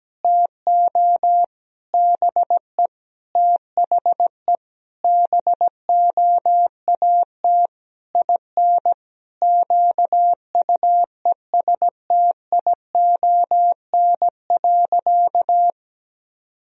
Morse code, heard as a keyed tone, about 17 wpm, 700 Hz. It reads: TO BE THE BOAT IN QUESTION.
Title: Sentences from Top 1000 Words - Review Part 3 - 17wpm